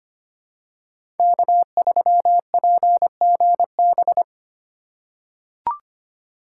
Morse code, heard as a keyed tone, about 25 wpm, 700 Hz. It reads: K3PGB
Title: USA Call Signs - 2x Character Spacing - 25wpm